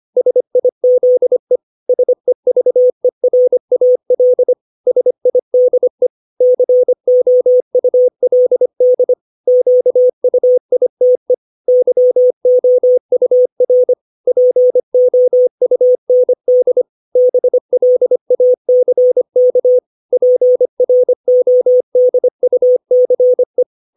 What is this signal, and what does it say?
Morse code, heard as a keyed tone, about 25 wpm, 500 Hz. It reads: SIZE SEVERAL SIDE COULD QUITE YOUR POUND BLACK PRODUCE
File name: 632 words / 632 Words 25WPM